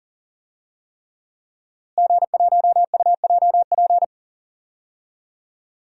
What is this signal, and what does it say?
Morse code, heard as a keyed tone, about 40 wpm, 700 Hz. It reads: G1UJP
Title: DXCC-100 Call Signs - 5 Characters - Repeated 3x - 40wpm